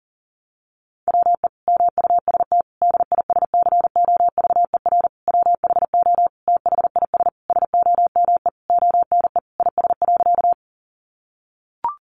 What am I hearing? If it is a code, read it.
Morse code, heard as a keyed tone, about 40 wpm, 700 Hz. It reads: WE MUST DISCOVER WHO THIS SOME ONE IS.
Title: Sentences from Top 1100 Words 40wpm